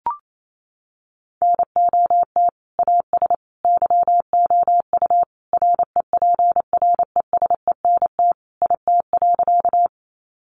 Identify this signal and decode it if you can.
Morse code, heard as a keyed tone, about 28 wpm, 700 Hz. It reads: NOT AS YOU REPRESENT IT.
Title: Sentences from Top 1200 Words 28wpm